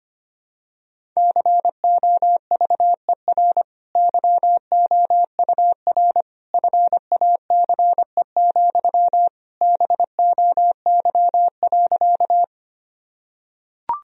Morse code, heard as a keyed tone, about 25 wpm, 700 Hz. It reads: COVER YOUR FACE, BOY.
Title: Sentences from Top 1000 Words - Review Part 1 - 25wpm